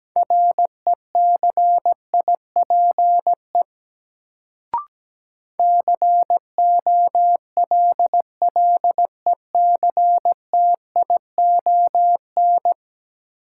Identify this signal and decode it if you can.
Morse code, heard as a keyed tone, about 17 wpm, 700 Hz. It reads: RECIPE COLLECTION
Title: Sentences from Top 1200 Words - New Word Drill Alt - 17wpm